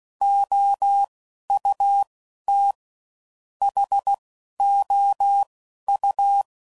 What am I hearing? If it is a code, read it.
Morse code, heard as a keyed tone, about 16 wpm, 800 Hz. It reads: O U T H O U
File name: Suffixes / General / Houses